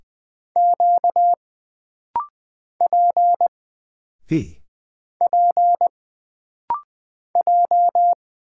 Morse code, heard as a keyed tone, about 20 wpm, 700 Hz. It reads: Q P P J
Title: Single Character (Letter:Number:Punctuation:Prosign) - Rapid-Fire - 20wpm